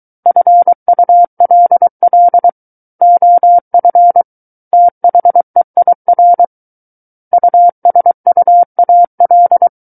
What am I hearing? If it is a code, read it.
Morse code, heard as a keyed tone, about 23 wpm, 700 Hz. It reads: FULL OF THEIR USUAL
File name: Book One / 14-Chapter 14 - 23 WPM